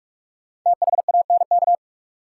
Morse code, heard as a keyed tone, about 44 wpm, 700 Hz. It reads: THANK
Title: Sentences from Top 600 Words - New Word Drill - 45wpm